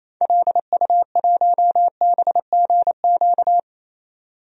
Morse code, heard as a keyed tone, about 28 wpm, 700 Hz. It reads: LU1BGQ
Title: DXCC-100 Call Signs - 6 Characters - 28wpm